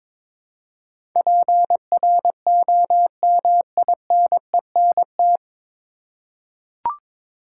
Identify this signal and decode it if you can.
Morse code, heard as a keyed tone, about 22 wpm, 700 Hz. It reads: PROMINENT
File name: 9 Letter Words - First 3 Letters Given - 22wpm